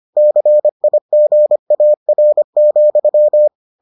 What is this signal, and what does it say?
Morse code, heard as a keyed tone, about 25 wpm, 600 Hz. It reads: CIGAR,